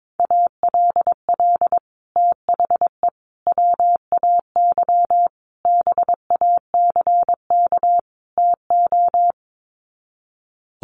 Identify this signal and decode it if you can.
Morse code, heard as a keyed tone, about 22 wpm, 700 Hz. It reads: ALL THE WAY BACK TO
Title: Sets of 5 Words Encore 22wpm